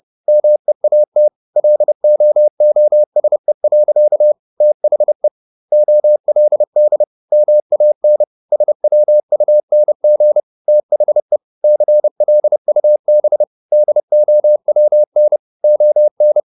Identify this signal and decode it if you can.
Morse code, heard as a keyed tone, about 30 wpm, 600 Hz. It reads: MEAT LOOSE. THE OLD MAN SWUNG THE CLUB DOWN ON